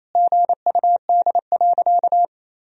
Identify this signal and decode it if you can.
Morse code, heard as a keyed tone, about 28 wpm, 700 Hz. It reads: GUD.
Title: Sentences from Top 500 Words and Abbreviations 28wpm